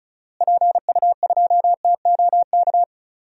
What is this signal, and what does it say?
Morse code, heard as a keyed tone, about 36 wpm, 700 Hz. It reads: PU2TOK
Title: DXCC-100 Call Signs - 6 Characters - Repeated 3x - 35wpm